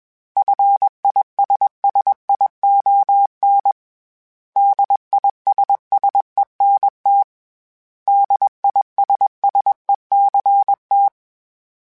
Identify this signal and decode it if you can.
Morse code, heard as a keyed tone, about 21 wpm, 800 Hz. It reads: FISSION DISSENT DISSECT
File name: Special INGs / Special INGs-mp3s / Suffixes @20